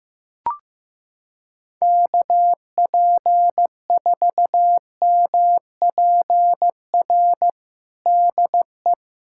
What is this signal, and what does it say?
Morse code, heard as a keyed tone, about 15 wpm, 700 Hz. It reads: KP4MPR DE
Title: USA Callsign DE Callsign - 6 Characters - 15wpm